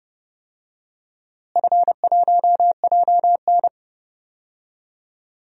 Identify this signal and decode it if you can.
Morse code, heard as a keyed tone, about 29 wpm, 700 Hz. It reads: F1JN
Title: DXCC-100 Call Signs - 4 Characters - Repeated 3x - 30wpm